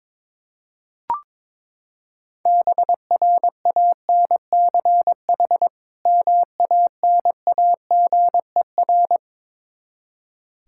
Morse code, heard as a keyed tone, about 21 wpm, 700 Hz. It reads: BRANCH MANAGER
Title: Practice QSO - Exchange 3 - More Info - 22wpm